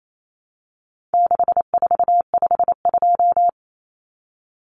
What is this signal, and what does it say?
Morse code, heard as a keyed tone, about 28 wpm, 700 Hz. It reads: 6452
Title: Common Cut Numbers - 4-Digit 28wpm